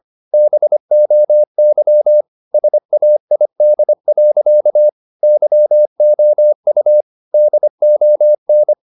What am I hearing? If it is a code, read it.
Morse code, heard as a keyed tone, about 25 wpm, 600 Hz. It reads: BOY SAID. YOU DON
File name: the old man and the sea part 1 25wpm